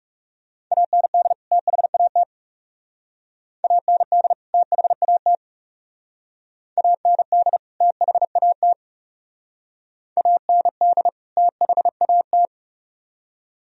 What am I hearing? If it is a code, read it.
Morse code, heard as a keyed tone, about 38 wpm, 700 Hz. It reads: AND THAT AND THAT AND THAT AND THAT